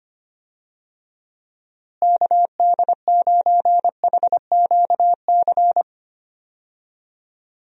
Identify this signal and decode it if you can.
Morse code, heard as a keyed tone, about 25 wpm, 700 Hz. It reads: KD9HQC
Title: USA Call Signs - 6 Characters - Repeated 3x - 25wpm